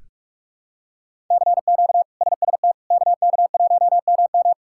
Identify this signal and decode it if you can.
Morse code, heard as a keyed tone, about 46 wpm, 700 Hz. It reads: CQ SST KK1MM
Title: K1USN SST QSO Practice - 45wpm